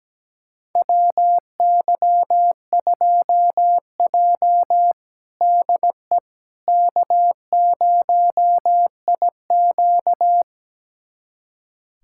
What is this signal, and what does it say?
Morse code, heard as a keyed tone, about 17 wpm, 700 Hz. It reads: WY2J DE K0IQ
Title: USA Callsign DE Callsign - 4 Characters - 17wpm